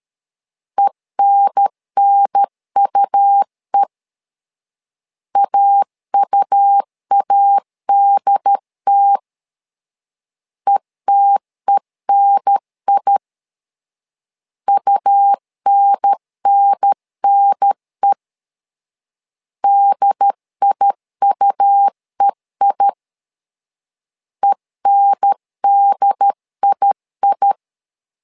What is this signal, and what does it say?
Morse code, heard as a keyed tone, about 13 wpm, 800 Hz. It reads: ENNUE AUADT ETENI UNNNE DIUEI ENDII
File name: LEZIONI / 2 UHDN / 2 UHDN 12wpm 1456